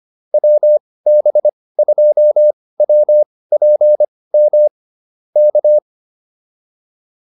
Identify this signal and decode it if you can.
Morse code, heard as a keyed tone, about 25 wpm, spaced as for 18 wpm, 600 Hz. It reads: WB2WPM K
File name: QSO106 18